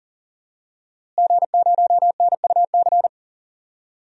Morse code, heard as a keyed tone, about 40 wpm, 700 Hz. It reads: G0NUC
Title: DXCC-100 Call Signs - 5 Characters - Repeated 2x - 40wpm